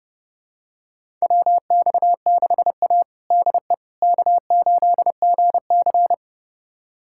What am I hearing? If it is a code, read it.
Morse code, heard as a keyed tone, about 30 wpm, 700 Hz. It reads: WX6A DE K8GC
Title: USA Callsign DE Callsign - 4 Characters - 30wpm